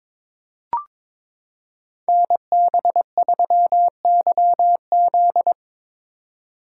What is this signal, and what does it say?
Morse code, heard as a keyed tone, about 22 wpm, 700 Hz. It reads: NB3YZ
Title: Code Groups - 5 Letter:Numbers - 22wpm